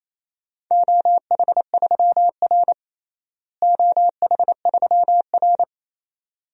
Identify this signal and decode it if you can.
Morse code, heard as a keyed tone, about 28 wpm, 700 Hz. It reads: OH3R OH3R
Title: DXCC-100 Call Signs - 4 Characters - Repeated 2x - 28wpm